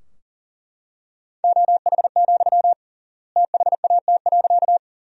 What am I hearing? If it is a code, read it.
Morse code, heard as a keyed tone, about 40 wpm, 700 Hz. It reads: OH, THAT.
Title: 3x Word-Spacing - Sentences from Top 500 Words - 40wpm